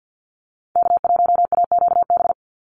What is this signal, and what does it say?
Morse code, heard as a keyed tone, about 50 wpm, 700 Hz. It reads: K1AQB
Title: Call Signs Repeated 2x - 50wpm